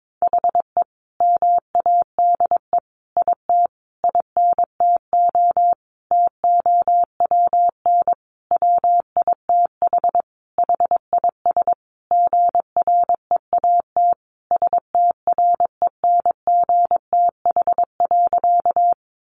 Morse code, heard as a keyed tone, about 22 wpm, 700 Hz. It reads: HE MADE IT INTO TOWN WITH HIS GREAT STRENGTH.
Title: Sentences from Top 1300 Words 22wpm